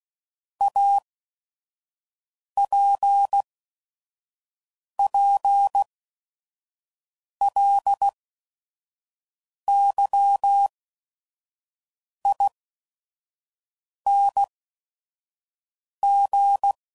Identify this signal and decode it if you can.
Morse code, heard as a keyed tone, about 16 wpm, spaced as for 5 wpm, 800 Hz. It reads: APPLYING
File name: CW Code Groups @5WPM